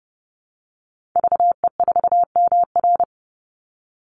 Morse code, heard as a keyed tone, about 30 wpm, 700 Hz. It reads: VE4MR
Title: Call Signs - 30wpm